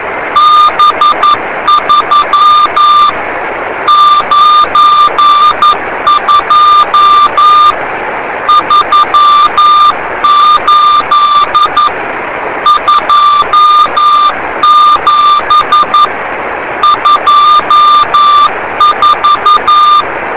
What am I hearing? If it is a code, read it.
Morse code, heard as a keyed tone, about 11 wpm, 1190 Hz, with noise. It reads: B3 92 38 27 24